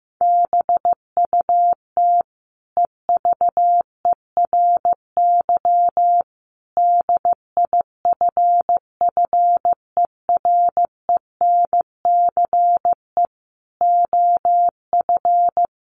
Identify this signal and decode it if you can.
Morse code, heard as a keyed tone, about 15 wpm, 700 Hz. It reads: BUT EVERY DIFFERENCE OF
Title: Sentences from Top 1400 Words 15wpm